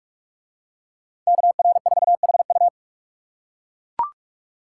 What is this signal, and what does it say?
Morse code, heard as a keyed tone, about 44 wpm, 700 Hz. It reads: KR4HU